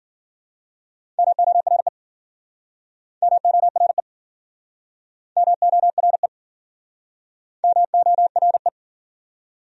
Morse code, heard as a keyed tone, about 52 wpm, 700 Hz. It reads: MORE MORE MORE MORE